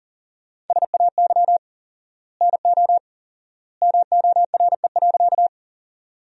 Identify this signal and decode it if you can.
Morse code, heard as a keyed tone, about 40 wpm, 700 Hz. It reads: SAY NO MORE.
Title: 4x Word-Spacing - Sentences from Top 200 Words - 40wpm